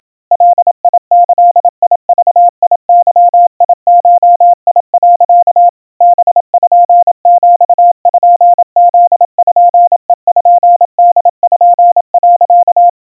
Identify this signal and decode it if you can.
Morse code (Japanese Wabun code), heard as a keyed tone, about 27 wpm, 700 Hz. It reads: ガギグゲゴ、 パピプペポ、
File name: iroha 90 cpm 700 Hz Crispy